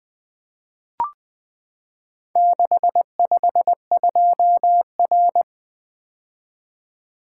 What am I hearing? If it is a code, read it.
Morse code, heard as a keyed tone, about 20 wpm, 700 Hz. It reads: 652R
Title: Four Letter-Number Combinations - 20wpm